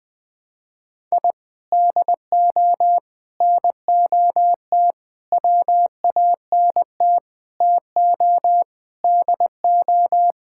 Morse code, heard as a keyed tone, about 20 wpm, 700 Hz. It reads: I DO NOT WANT TO DO
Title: Sentences with 5-Word Phrases - 20wpm